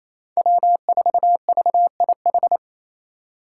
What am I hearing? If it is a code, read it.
Morse code, heard as a keyed tone, about 27 wpm, 700 Hz. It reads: W4VIH
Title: Code Groups - 5 Letter:Numbers - 28wpm